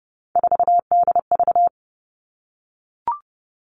Morse code, heard as a keyed tone, about 30 wpm, 700 Hz. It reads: 4DV
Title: Three Challenging Letter-Number Combinations 30wpm